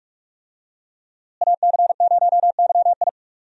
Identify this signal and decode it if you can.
Morse code, heard as a keyed tone, about 41 wpm, 700 Hz. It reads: AC0YI
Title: USA Call Signs - 5 Characters - Repeated 2x - 45wpm